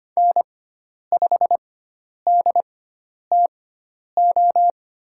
Morse code, heard as a keyed tone, about 25 wpm, spaced as for 10 wpm, 700 Hz. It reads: N5DTO
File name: Lesson 16g - Letter c - Call Signs 25wpm